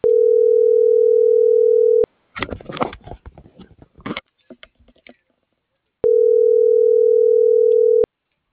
Ringback tone (North America), 440 + 480 Hz.